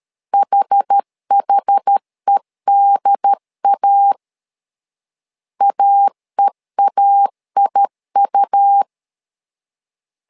Morse code, heard as a keyed tone, about 13 wpm, 800 Hz. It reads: HHEDA AEAIU
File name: LEZIONI / 2 UHDN / 2 UHD 12wpm 1540